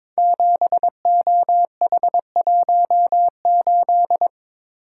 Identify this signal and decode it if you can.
Morse code, heard as a keyed tone, about 22 wpm, 700 Hz. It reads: 7OH18
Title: Code Groups - 5 Letter:Numbers - 22wpm